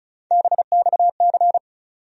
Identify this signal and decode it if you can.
Morse code, heard as a keyed tone, about 34 wpm, 700 Hz. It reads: BXC
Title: USA Call Sign Postfixes - 3 Letters 35wpm